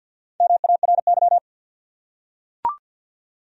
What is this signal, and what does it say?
Morse code, heard as a keyed tone, about 45 wpm, 700 Hz. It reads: MARY
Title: USA Common Names - Max 4 Letters - 50wpm